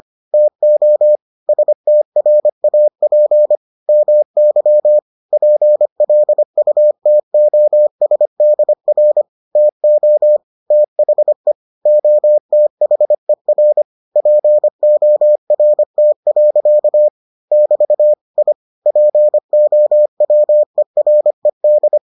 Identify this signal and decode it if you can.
Morse code, heard as a keyed tone, about 25 wpm, 600 Hz. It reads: TO STRAP MY PLUTOSDR TO THE OTHER PORT. <BT> I POWERED